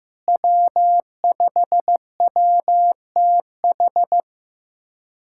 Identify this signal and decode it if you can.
Morse code, heard as a keyed tone, about 15 wpm, 700 Hz. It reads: W5WTH
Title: K1USN SST QSO Practice - 15wpm